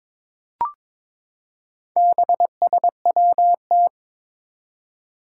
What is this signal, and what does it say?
Morse code, heard as a keyed tone, about 22 wpm, 700 Hz. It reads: BSWT
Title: Four Challenging Letter-Number Combinations 22wpm